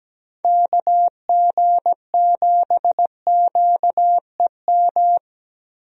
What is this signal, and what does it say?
Morse code, heard as a keyed tone, about 17 wpm, 700 Hz. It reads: KG7QEM
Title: USA Call Signs - 6 Characters - Repeated 3x - 17wpm